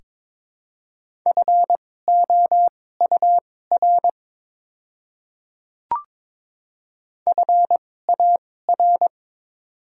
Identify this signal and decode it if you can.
Morse code, heard as a keyed tone, about 22 wpm, spaced as for 16 wpm, 700 Hz. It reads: FOUR FAR
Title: 2x Character Spacing - Top 300 Words - 22wpm